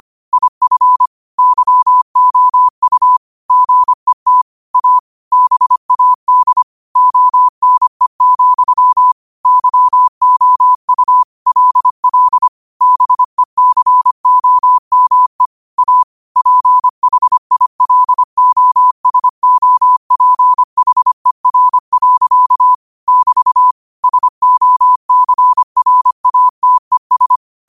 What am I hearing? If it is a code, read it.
Morse code, heard as a keyed tone, about 25 wpm, 1000 Hz. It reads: IF YOU GET A BAD ONE, YOU LL BECOME A PHILOSOPHER. <BT> SOCRATES
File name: 25wpm